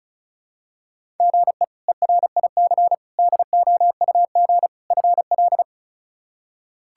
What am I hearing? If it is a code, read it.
Morse code, heard as a keyed tone, about 34 wpm, 700 Hz. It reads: GE ERIC DOUG FL
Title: K1USN SST QSO Practice - 35wpm